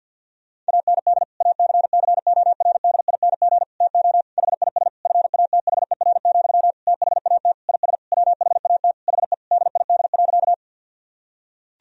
Morse code, heard as a keyed tone, about 50 wpm, 700 Hz. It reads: AND ACCORDING TO HIS FATHER, THAT IS WHAT HE DID.
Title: Sentences from Top 1000 Words - Review Part 4 - 50wpm